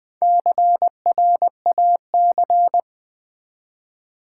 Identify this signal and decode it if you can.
Morse code, heard as a keyed tone, about 20 wpm, 700 Hz. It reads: CRAC
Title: Build A Word - 1 Letter At A Time - 5-Letter Words - 20wpm